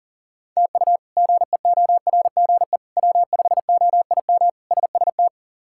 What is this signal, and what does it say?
Morse code, heard as a keyed tone, about 40 wpm, 700 Hz. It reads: TU GEORGE W5OIM SST
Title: K1USN SST QSO Practice - 40wpm